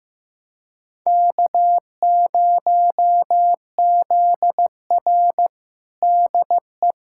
Morse code, heard as a keyed tone, about 15 wpm, 700 Hz. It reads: K0ZR DE
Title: Practice QSO - Exchange 4 - Ending - 15wpm